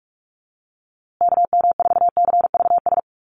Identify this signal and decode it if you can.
Morse code, heard as a keyed tone, about 44 wpm, 690 Hz. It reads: KM4CVS
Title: USA Call Signs Repeated 3x 45wpm